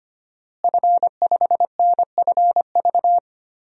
Morse code, heard as a keyed tone, about 25 wpm, 700 Hz. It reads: F5NFV
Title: DXCC-100 Call Signs - 5 Characters - Repeated 3x - 25wpm